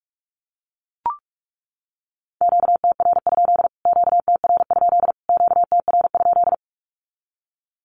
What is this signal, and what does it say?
Morse code, heard as a keyed tone, about 44 wpm, 700 Hz. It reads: QTR? QTR? QTR?
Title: Q-Codes Repeated 3x 45wpm